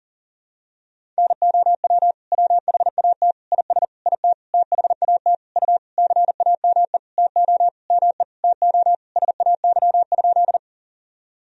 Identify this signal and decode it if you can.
Morse code, heard as a keyed tone, about 40 wpm, 700 Hz. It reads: NOW WHAT IS IT THAT U CAME TO ME TO SAY?
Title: Sentences from Top 200 Words and Abbreviations 40wpm